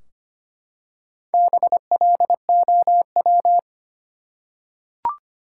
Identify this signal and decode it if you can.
Morse code, heard as a keyed tone, about 25 wpm, 700 Hz. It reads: BLOW
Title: Four Letter Words - 25wpm